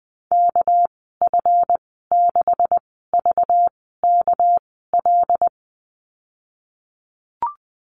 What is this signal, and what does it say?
Morse code, heard as a keyed tone, about 20 wpm, spaced as for 15 wpm, 700 Hz. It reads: KF6VKL
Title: 238 Lesson 38g - Letter q - Call Signs 20wpm